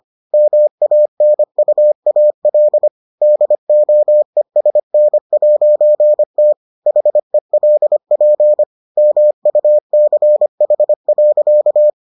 Morse code, heard as a keyed tone, about 25 wpm, 600 Hz. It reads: MANUAL DOESN'T HELP MUCH.